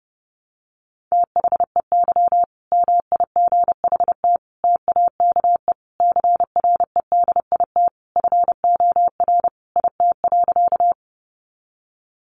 Morse code, heard as a keyed tone, about 30 wpm, 700 Hz. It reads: THEY MIGHT TAKE CREDIT FOR IT.
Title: Sentences from Top 1000 Words 30wpm